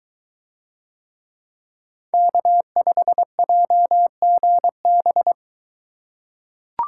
Morse code, heard as a keyed tone, about 22 wpm, 700 Hz. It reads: K5JGB